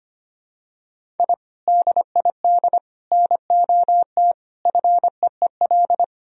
Morse code, heard as a keyed tone, about 25 wpm, 700 Hz. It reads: I DID NOT FEEL
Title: Sentences from Top 1000 Words - Review Part 1 - 25wpm